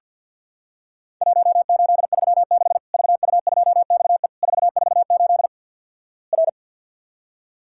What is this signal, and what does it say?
Morse code, heard as a keyed tone, about 50 wpm, 700 Hz. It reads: 1836 VU2XE 448 R
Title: Worked All Europe DX Contest - QTC Exchange with Repeats - 50wpm